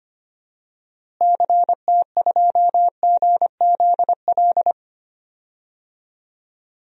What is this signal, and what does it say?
Morse code, heard as a keyed tone, about 25 wpm, 700 Hz. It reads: CT2GZL
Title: DXCC-100 Call Signs - 6 Characters - Repeated 2x - 25wpm